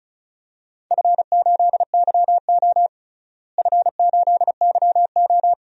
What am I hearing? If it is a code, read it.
Morse code, heard as a keyed tone, about 36 wpm, 700 Hz. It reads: F8YO F8YO
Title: DXCC-100 Call Signs - 4 Characters - Repeated 2x - 35wpm